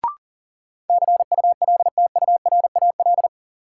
Morse code, heard as a keyed tone, about 40 wpm, 700 Hz. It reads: CULTURAL
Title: Sentences from Top 1500 Words - New Word Drill - 40wpm